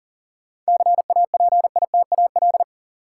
Morse code, heard as a keyed tone, about 39 wpm, 700 Hz. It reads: CAPITAL
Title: Sentences from Top 1000 Words - New Word Drill - 40wpm